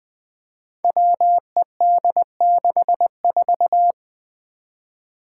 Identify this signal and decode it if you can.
Morse code, heard as a keyed tone, about 20 wpm, 700 Hz. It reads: WED64
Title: Five Challenging Letter-Number Combinations 20wpm